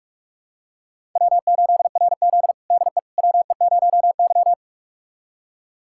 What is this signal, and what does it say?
Morse code, heard as a keyed tone, about 41 wpm, 700 Hz. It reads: W8RZ DE WE0Y
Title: USA Callsign DE Callsign - 4 Characters - 45wpm